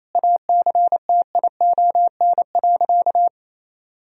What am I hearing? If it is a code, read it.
Morse code, heard as a keyed tone, about 28 wpm, 700 Hz. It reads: ACTION.
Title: Sentences from Top 1000 Words - Review Part 4 - 28wpm